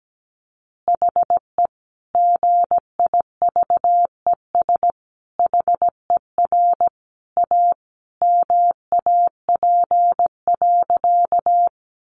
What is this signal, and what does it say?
Morse code, heard as a keyed tone, about 17 wpm, 700 Hz. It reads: HE GIVES HER A MAP.
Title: Additional Easy Phrases 17wpm